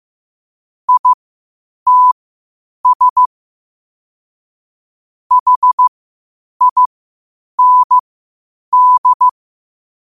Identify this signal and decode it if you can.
Morse code, heard as a keyed tone, about 15 wpm, spaced as for 8 wpm, 1000 Hz. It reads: ITS HIND
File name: qotdfiles / 8wpm